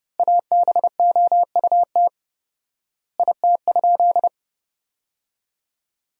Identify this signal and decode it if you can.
Morse code, heard as a keyed tone, about 30 wpm, 700 Hz. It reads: ABOUT IT?
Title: 4x Word-Spacing - Sentences from Top 100 Words - 30wpm